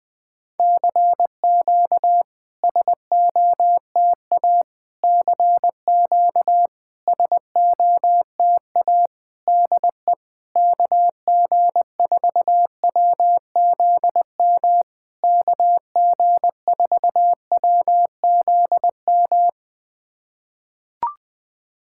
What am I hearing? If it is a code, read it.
Morse code, heard as a keyed tone, about 21 wpm, 700 Hz. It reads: CQ SOTA CQ SOTA DE KG4WZM KG4WZM
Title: SOTA QSO Practice 20wpm